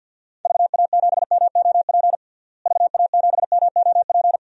Morse code, heard as a keyed tone, about 47 wpm, 700 Hz. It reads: VA7MOP VA7MOP
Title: DXCC-100 Call Signs - 6 Characters - Repeated 3x - 50wpm